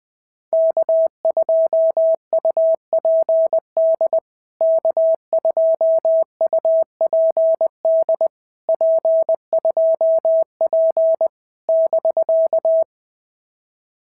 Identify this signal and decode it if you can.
Morse code, heard as a keyed tone, about 20 wpm, 650 Hz. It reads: K2UPD K2UPD P2P <BK>
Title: POTA QSO Practice - 20wpm